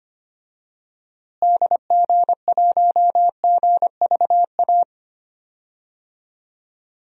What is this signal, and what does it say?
Morse code, heard as a keyed tone, about 25 wpm, 700 Hz. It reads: DG1GVA
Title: DXCC-100 Call Signs - 6 Characters - Repeated 2x - 25wpm